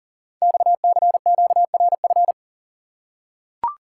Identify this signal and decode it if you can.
Morse code, heard as a keyed tone, about 40 wpm, 700 Hz. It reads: XCQRF
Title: Code Groups - 5 Letters - 40wpm